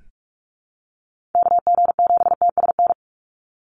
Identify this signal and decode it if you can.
Morse code, heard as a keyed tone, about 44 wpm, 700 Hz. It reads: KG7TSN
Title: USA Call Signs 45wpm